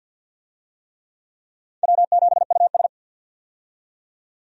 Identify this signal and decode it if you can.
Morse code, heard as a keyed tone, about 53 wpm, 700 Hz. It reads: W7US
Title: USA Call Signs - 4 Characters - 50wpm